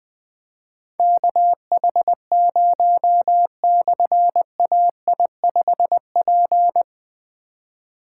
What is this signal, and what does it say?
Morse code, heard as a keyed tone, about 20 wpm, 700 Hz. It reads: KH0/AI5P
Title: Difficult Call Signs 20wpm